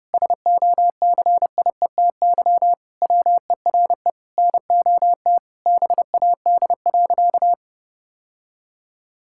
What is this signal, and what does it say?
Morse code, heard as a keyed tone, about 30 wpm, 700 Hz. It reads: SOCIETY WERE NOT BAD.
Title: Sentences from Top 1000 Words - Review Part 5 - 30wpm